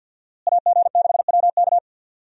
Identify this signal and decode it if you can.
Morse code, heard as a keyed tone, about 52 wpm, 700 Hz. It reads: AG6WK
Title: USA Call Signs - 5 Characters - Repeated 3x - 50wpm